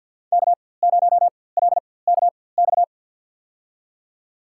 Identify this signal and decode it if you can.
Morse code, heard as a keyed tone, about 47 wpm, spaced as for 23 wpm, 700 Hz. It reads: K0LKX
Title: USA Call Signs - 5 Characters - 4x Character Spacing - 50wpm